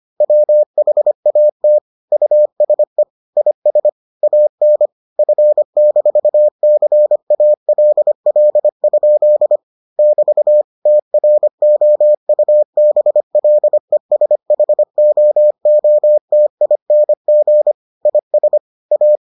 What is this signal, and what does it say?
Morse code, heard as a keyed tone, about 25 wpm, 600 Hz. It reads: WHAT USE IS AN F-CALL? <BT> TROUBLESHOOTING IS A